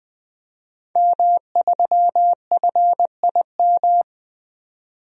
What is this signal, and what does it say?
Morse code, heard as a keyed tone, about 20 wpm, 700 Hz. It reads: M3FIM
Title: DXCC-100 Call Signs - 5 Characters - 20wpm